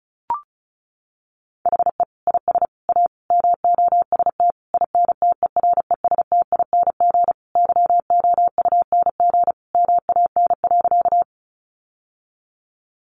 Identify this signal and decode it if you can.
Morse code, heard as a keyed tone, about 34 wpm, 700 Hz. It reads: HE IS A MOST INTERESTING YOUNG MAN.
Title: Sentences from Top 1000 Words 35wpm